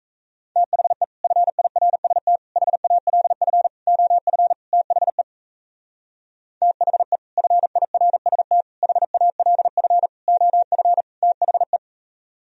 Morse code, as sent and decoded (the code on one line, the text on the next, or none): - .... / ..-. .. .-. ... - / .... .- .-.. ..-. / --- ..-. / - .... / - .... / ..-. .. .-. ... - / .... .- .-.. ..-. / --- ..-. / - .... .
THE FIRST HALF OF THE THE FIRST HALF OF THE